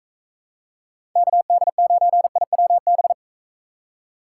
Morse code, heard as a keyed tone, about 40 wpm, 700 Hz. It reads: KD9IWB